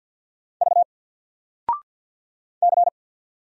Morse code, heard as a keyed tone, about 50 wpm, 700 Hz. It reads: V C
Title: Single Character (Letter:Number:Punctuation:Prosign) 50wpm